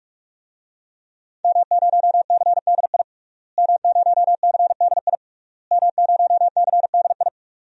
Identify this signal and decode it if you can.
Morse code, heard as a keyed tone, about 42 wpm, 700 Hz. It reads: M0CDI M0CDI M0CDI
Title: DXCC-100 Call Signs - 5 Characters - Repeated 3x - 45wpm